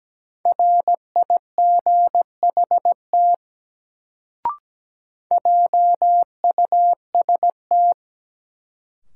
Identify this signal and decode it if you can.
Morse code, heard as a keyed tone, about 17 wpm, 700 Hz. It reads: RIGHT JUST
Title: Sentences from Top 200 Words - New Word Drill - 17wpm